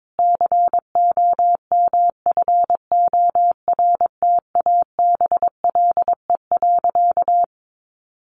Morse code, heard as a keyed tone, about 22 wpm, 700 Hz. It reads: COMFORTABLE.
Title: Sentences from Top 1500 Words 22wpm